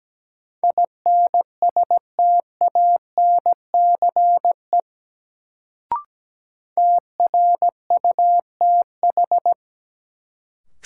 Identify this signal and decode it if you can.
Morse code, heard as a keyed tone, about 17 wpm, 700 Hz. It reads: INSTANCE TRUTH
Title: Sentences from Top 800 Words - New Word Drill - 17wpm